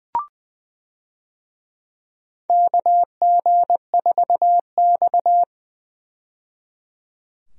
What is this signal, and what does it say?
Morse code, heard as a keyed tone, about 20 wpm, 700 Hz. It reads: KG4X
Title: USA Call Signs - 4 Characters - 20wpm